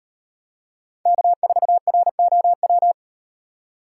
Morse code, heard as a keyed tone, about 36 wpm, 700 Hz. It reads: K4ROW